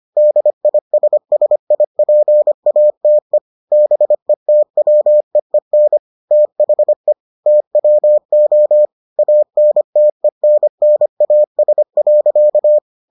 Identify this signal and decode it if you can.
Morse code, heard as a keyed tone, about 25 wpm, 600 Hz. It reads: DISSIPATE BETWEEN THE TWO ANTENNAS.